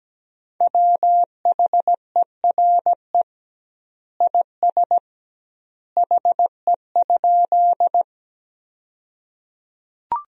Morse code, heard as a keyed tone, about 17 wpm, 700 Hz. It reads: WHERE IS HE?
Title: 2x Word-Spacing - Sentences from Top 200 Words - 17wpm